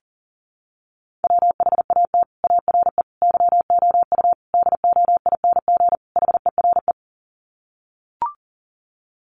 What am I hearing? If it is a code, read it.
Morse code, heard as a keyed tone, about 37 wpm, 700 Hz. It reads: WHAT ARE YOU DOING HERE
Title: Sets of 5 Words 40wpm